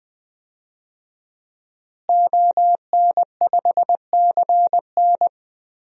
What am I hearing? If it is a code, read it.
Morse code, heard as a keyed tone, about 20 wpm, 700 Hz. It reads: ON5CN
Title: DXCC-100 Call Signs - 5 Characters - Repeated 3x - 20wpm